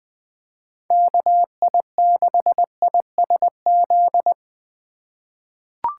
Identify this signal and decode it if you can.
Morse code, heard as a keyed tone, about 20 wpm, 700 Hz. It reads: KI6ISZ
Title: USA Call Signs - 3x Character Spacing - 20wpm